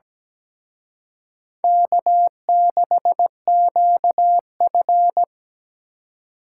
Morse code, heard as a keyed tone, about 20 wpm, 700 Hz. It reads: K6QF
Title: Speed Racing - USA 1x2 Call Signs 17wpm